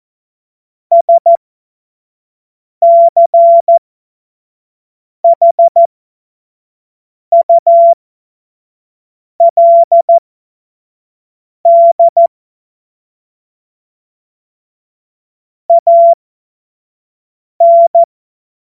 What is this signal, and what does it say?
Morse code, heard as a keyed tone, about 14 wpm, spaced as for 5 wpm, 680 Hz. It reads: SCHULD AN
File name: Morsekurs / Witze / Witz 15. November 2022 25 F70